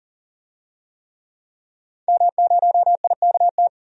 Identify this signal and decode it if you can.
Morse code, heard as a keyed tone, about 40 wpm, 700 Hz. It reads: M0IKT
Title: DXCC-100 Call Signs - 5 Characters - 40wpm